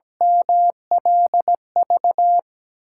Morse code, heard as a keyed tone, about 17 wpm, 700 Hz. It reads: MLV
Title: USA Call Signs - 8x Character Spacing - 17wpm